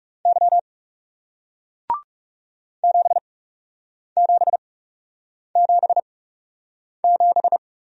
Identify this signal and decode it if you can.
Morse code, heard as a keyed tone, about 40 wpm, spaced as for 8 wpm, 700 Hz. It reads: Y 7777